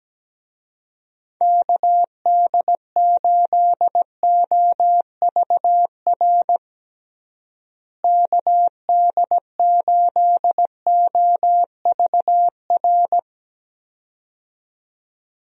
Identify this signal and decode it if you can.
Morse code, heard as a keyed tone, about 17 wpm, 700 Hz. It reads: KD8OVR KD8OVR
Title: USA Call Signs - 6 Characters - Repeated 2x - 17wpm